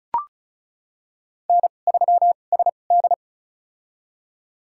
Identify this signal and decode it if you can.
Morse code, heard as a keyed tone, about 36 wpm, spaced as for 25 wpm, 700 Hz. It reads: N3SD
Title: USA Call Signs - 4 Characters - 2x Character Spacing - 35wpm